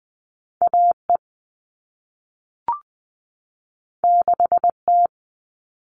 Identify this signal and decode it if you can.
Morse code, heard as a keyed tone, about 20 wpm, 700 Hz. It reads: A E 6 T
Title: All Cut Numbers - 2-Digit 20wpm